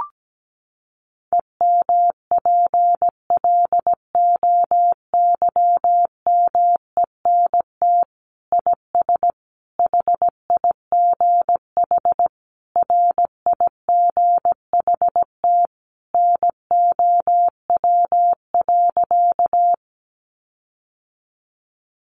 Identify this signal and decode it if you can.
Morse code, heard as a keyed tone, about 17 wpm, 700 Hz. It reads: EMPLOYMENT IS HIGH RIGHT NOW.
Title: Sentences from Top 1500 Words 17wpm